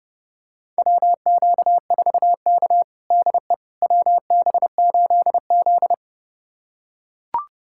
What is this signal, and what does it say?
Morse code, heard as a keyed tone, about 30 wpm, 700 Hz. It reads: WQ4K DE WB8Z
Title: USA Callsign DE Callsign - 4 Characters - 30wpm